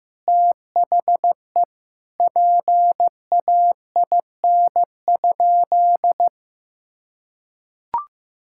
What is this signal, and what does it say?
Morse code, heard as a keyed tone, about 16 wpm, 700 Hz. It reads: THE PAIN?
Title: Sentences from Top 1600 Words 15wpm